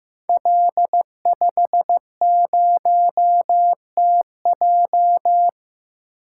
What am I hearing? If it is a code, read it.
Morse code, heard as a keyed tone, about 16 wpm, 700 Hz. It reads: L50TJ
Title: Code Groups - 5 Letter:Numbers - 15wpm